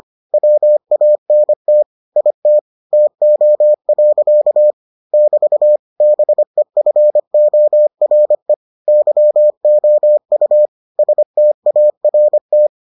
Morse code, heard as a keyed tone, about 25 wpm, 600 Hz. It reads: WANT IT TO. <BT> BEFORE YOU START